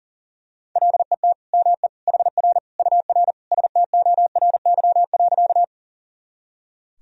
Morse code, heard as a keyed tone, about 40 wpm, 700 Hz. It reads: LET ME HR UR STORY.
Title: Sentences from Top 300 Words and Abbreviations 40wpm